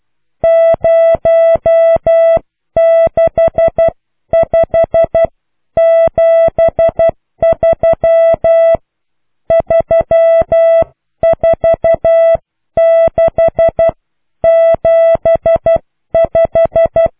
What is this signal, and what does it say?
Morse code, heard as a keyed tone, about 12 wpm, spaced as for 10 wpm, 660 Hz. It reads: 06573 34675